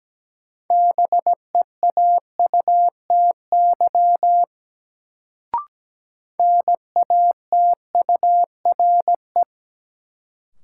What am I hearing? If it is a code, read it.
Morse code, heard as a keyed tone, about 17 wpm, 700 Hz. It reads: BEAUTY NATURE
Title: Sentences from Top 600 Words - New Word Drill - 17wpm